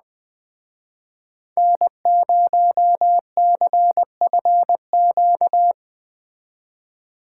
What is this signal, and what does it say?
Morse code, heard as a keyed tone, about 20 wpm, 700 Hz. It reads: N0CFQ